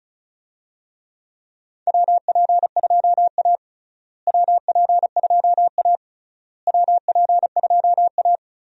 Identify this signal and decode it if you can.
Morse code, heard as a keyed tone, about 34 wpm, 700 Hz. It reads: WP2A WP2A WP2A
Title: USA Call Signs - 4 Characters - Repeated 3x - 35wpm